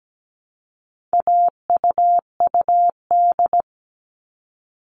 Morse code, heard as a keyed tone, about 17 wpm, 700 Hz. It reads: AUUD
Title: All Cut Numbers - 4-Digit 17wpm